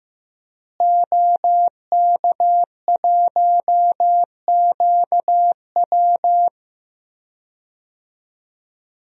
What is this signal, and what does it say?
Morse code, heard as a keyed tone, about 15 wpm, 700 Hz. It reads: OK1QW
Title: DXCC-100 Call Signs - 5 Characters - Repeated 2x - 15wpm